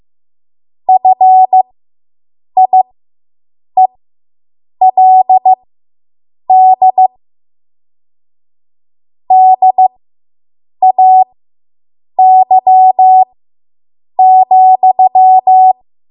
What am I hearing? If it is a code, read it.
Morse code, heard as a keyed tone, about 15 wpm, spaced as for 7 wpm, 750 Hz. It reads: FIELD DAY,